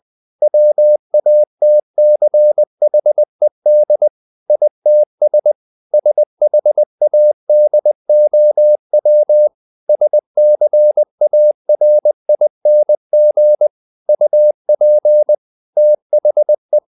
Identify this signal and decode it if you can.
Morse code, heard as a keyed tone, about 20 wpm, 600 Hz. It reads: WATCHED ITS SHADOW SCARING UP THE
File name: the old man and the sea part 2 20wpm